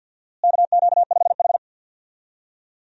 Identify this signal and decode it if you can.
Morse code, heard as a keyed tone, about 51 wpm, 700 Hz. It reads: KQ5H
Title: USA Call Signs - 4 Characters - Repeated 3x - 50wpm